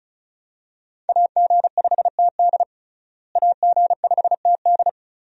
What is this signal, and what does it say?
Morse code, heard as a keyed tone, about 36 wpm, 700 Hz. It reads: AG5TD AG5TD
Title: USA Call Signs - 5 Characters - Repeated 2x - 35wpm